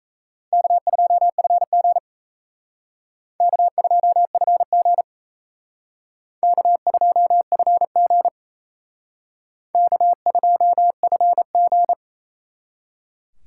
Morse code, heard as a keyed tone, about 36 wpm, 700 Hz. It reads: K2FG K2FG K2FG K2FG